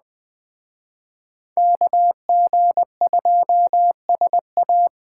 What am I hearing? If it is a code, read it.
Morse code, heard as a keyed tone, about 21 wpm, 700 Hz. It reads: KG2SA
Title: Speed Racing - USA 2x2 Call Signs 17wpm